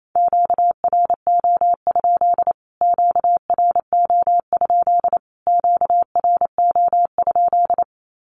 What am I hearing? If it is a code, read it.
Morse code, heard as a keyed tone, about 28 wpm, 700 Hz. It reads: QRO? QRO? QRO?
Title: Q-Codes Repeated 3x 28wpm